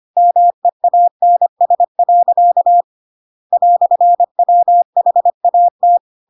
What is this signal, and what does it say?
Morse code, heard as a keyed tone, about 25 wpm, 700 Hz. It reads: MEANS. "WHAT